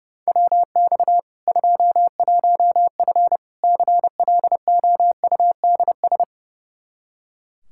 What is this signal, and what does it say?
Morse code, heard as a keyed tone, about 30 wpm, 700 Hz. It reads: WX 21F CLOUDS
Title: K1USN SST QSO Practice 30wpm